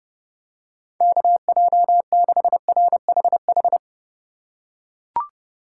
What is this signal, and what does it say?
Morse code, heard as a keyed tone, about 29 wpm, 700 Hz. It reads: KJ6RHH
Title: USA Call Signs - 6 Characters - 4x Character Spacing - 30wpm